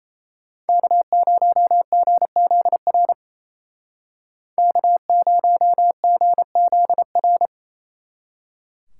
Morse code, heard as a keyed tone, about 34 wpm, 700 Hz. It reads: K0GZR K0GZR
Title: Speed Racing - USA 1x3 Call Signs 28wpm